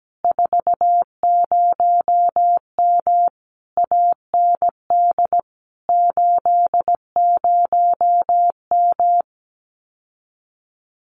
Morse code, heard as a keyed tone, about 17 wpm, 700 Hz. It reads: 40M AND 80M
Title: Common QSO Phrases Encore 17wpm